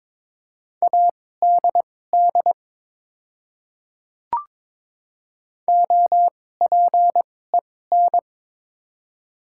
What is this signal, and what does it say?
Morse code, heard as a keyed tone, about 22 wpm, spaced as for 16 wpm, 690 Hz. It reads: ADD OPEN
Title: 2x Character Spacing - Top 300 Words - 22wpm